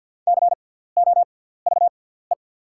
Morse code, heard as a keyed tone, about 48 wpm, spaced as for 17 wpm, 690 Hz. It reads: COVE
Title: 6x Character Spacing - Top 300 Words - 50wpm